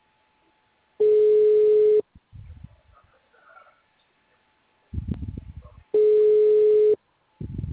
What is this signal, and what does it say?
Ringback tone (Europe, 425 Hz).